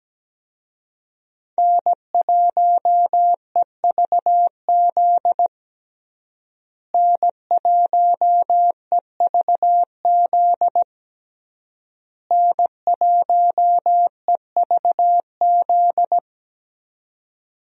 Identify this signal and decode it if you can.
Morse code, heard as a keyed tone, about 17 wpm, 700 Hz. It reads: N1EVZ N1EVZ N1EVZ
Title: USA Call Signs - 5 Characters - Repeated 3x - 17wpm